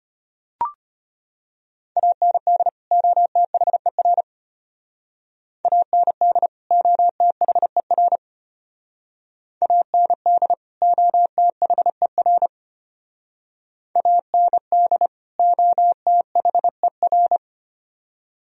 Morse code, heard as a keyed tone, about 33 wpm, 700 Hz. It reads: AND OTHER AND OTHER AND OTHER AND OTHER